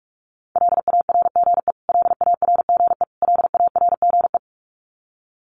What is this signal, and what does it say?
Morse code, heard as a keyed tone, about 44 wpm, 700 Hz. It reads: LARGE LARGE LARGE
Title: Top 500 Words Repeated 45wpm